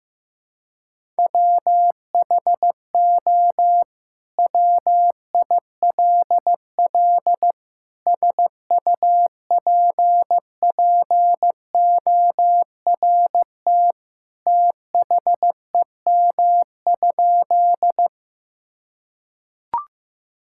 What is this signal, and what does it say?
Morse code, heard as a keyed tone, about 15 wpm, 700 Hz. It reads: WHO WILL SUPPORT THEM?
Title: Sentences from Top 1000 Words - Review Part 5 - 15wpm